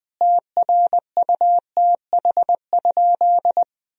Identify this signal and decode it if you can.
Morse code, heard as a keyed tone, about 21 wpm, 700 Hz. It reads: TRUTH?
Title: Sentences from Top 1000 Words - Review Part 4 - 20wpm